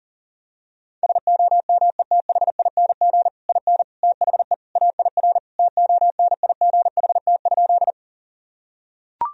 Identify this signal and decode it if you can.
Morse code, heard as a keyed tone, about 37 wpm, 700 Hz. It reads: SOMETHING IN THE AIR TONIGHT?
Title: Sentences from Top 1000 Words - Review Part 1 - 40wpm